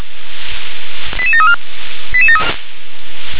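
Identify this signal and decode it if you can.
Five-tone selective call (ZVEI), heard as quick runs of five tones, tones of about 70 ms. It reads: 88734
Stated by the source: Five-tone ZVEI selective call received by SDR.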